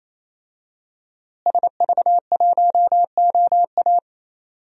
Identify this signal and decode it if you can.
Morse code, heard as a keyed tone, about 27 wpm, 700 Hz. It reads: SV1OA
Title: DXCC-100 Call Signs - 5 Characters - Repeated 3x - 28wpm